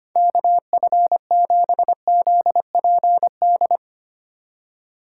Morse code, heard as a keyed tone, about 25 wpm, 700 Hz. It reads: KF7ZPD
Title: USA Call Signs - 6 Characters - 4x Character Spacing - 25wpm